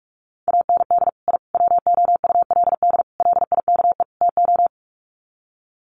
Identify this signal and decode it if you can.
Morse code, heard as a keyed tone, about 44 wpm, 700 Hz. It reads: AND I WOULD LIKE TO
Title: Sets of 5 Words Encore 45wpm